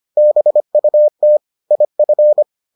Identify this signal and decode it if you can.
Morse code, heard as a keyed tone, about 25 wpm, 600 Hz. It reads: BUT IF